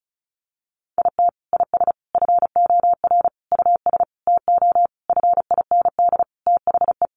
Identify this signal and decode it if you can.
Morse code, heard as a keyed tone, about 34 wpm, 700 Hz. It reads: IT IS FOR US TO FIND THE
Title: Sentences from Top 1300 Words 35wpm